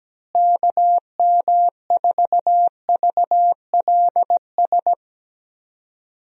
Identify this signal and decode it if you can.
Morse code, heard as a keyed tone, about 17 wpm, 700 Hz. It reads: KM4VLS
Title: USA Call Signs - 6 Characters - Repeated 3x - 17wpm